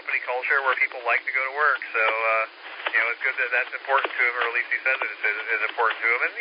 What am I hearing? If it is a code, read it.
Emergency Alert System (SAME) tones (header bursts), heard as harsh buzzing bursts, with speech. It is heard from 1.2 to 5.6 s.